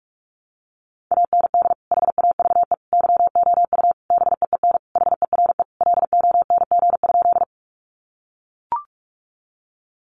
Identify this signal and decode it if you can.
Morse code, heard as a keyed tone, about 44 wpm, 700 Hz. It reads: AND HAVE YOU BEEN HERE LONG?
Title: Sentences from Top 200 Words 45wpm